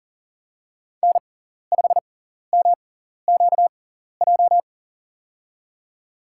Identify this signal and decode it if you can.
Morse code, heard as a keyed tone, about 40 wpm, spaced as for 14 wpm, 700 Hz. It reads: N5MQJ
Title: USA Call Signs - 6x Character Spacing - 40wpm